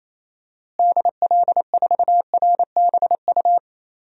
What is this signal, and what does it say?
Morse code, heard as a keyed tone, about 28 wpm, 700 Hz. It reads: DL4RBU
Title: DXCC-100 Call Signs - 6 Characters - Repeated 2x - 28wpm